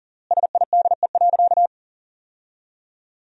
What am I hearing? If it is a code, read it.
Morse code, heard as a keyed tone, about 40 wpm, 700 Hz. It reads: SIDE.
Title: 4x Word-Spacing - Sentences from Top 400 Words - 40wpm